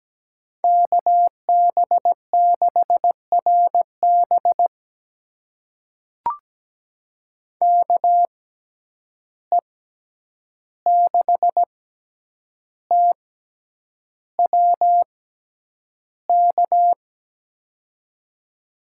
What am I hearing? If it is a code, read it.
Morse code, heard as a keyed tone, about 17 wpm, spaced as for 8 wpm, 700 Hz. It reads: KB6RB KE6TWK
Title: USA Call Signs - 6x Character Spacing - 17wpm